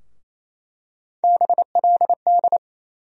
Morse code, heard as a keyed tone, about 28 wpm, 700 Hz. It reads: BLD
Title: USA Call Sign Postfixes - 3 Letters 28wpm